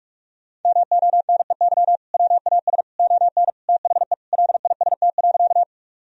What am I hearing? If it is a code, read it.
Morse code, heard as a keyed tone, about 41 wpm, 700 Hz. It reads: MONEY WAS ON THE LIST.
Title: Sentences from Top 1000 Words - Review Part 2 - 45wpm